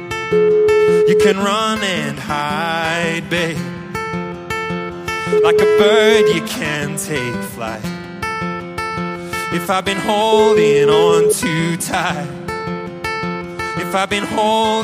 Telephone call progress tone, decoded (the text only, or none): ringback tone